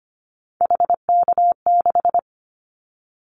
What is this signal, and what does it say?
Morse code, heard as a keyed tone, about 25 wpm, 700 Hz. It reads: HK6
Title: Callsign Prefixes for Top 100 DXCC - 25wpm